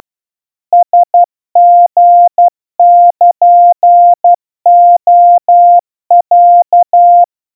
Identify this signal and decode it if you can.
Morse code (Japanese Wabun code), heard as a keyed tone, about 12 wpm, 700 Hz. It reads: ラリルレロ
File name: aiueo 40 cpm 700 Hz Crispy